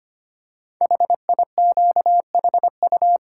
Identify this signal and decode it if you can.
Morse code, heard as a keyed tone, about 25 wpm, 700 Hz. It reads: HIQHU
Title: Code Groups - 5 Letters - 25wpm